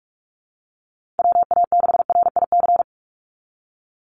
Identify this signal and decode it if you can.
Morse code, heard as a keyed tone, about 44 wpm, 700 Hz. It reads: WA6RIC
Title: NA SOTA Chaser Callsigns 45wpm